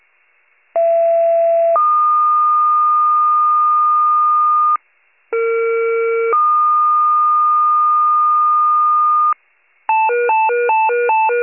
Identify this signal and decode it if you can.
Two-tone page, heard as a one-second tone, then a three-second tone, about 670 then 1220 Hz, about 460 then 1220 Hz. It is heard from 0.7 to 9.3 s.
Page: 39 RESCUE